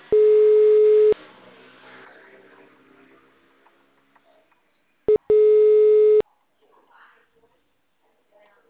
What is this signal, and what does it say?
Ringback tone (Europe, 425 Hz).
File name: out- -2009- - - .1879